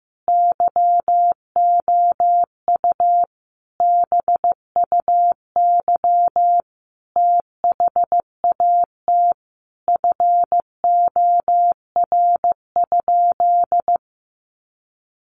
Morse code, heard as a keyed tone, about 15 wpm, 700 Hz. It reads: YOU BUY THAT FOR?
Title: Sentences from Top 700 Words 15wpm